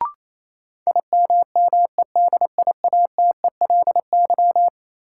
Morse code, heard as a keyed tone, about 28 wpm, 700 Hz. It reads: IMMEDIATELY
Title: Sentences from Top 1100 Words - New Word Drill - 28wpm